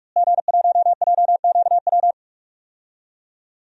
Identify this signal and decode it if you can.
Morse code, heard as a keyed tone, about 46 wpm, 700 Hz. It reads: G1JQW
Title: DXCC-100 Call Signs - 5 Characters - Repeated 3x - 45wpm